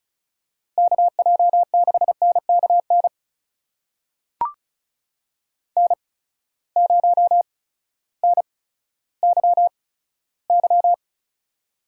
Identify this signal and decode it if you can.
Morse code, heard as a keyed tone, about 36 wpm, spaced as for 16 wpm, 700 Hz. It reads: KJ6NKN N0NYY
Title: USA Call Signs - 8x Character Spacing - 35wpm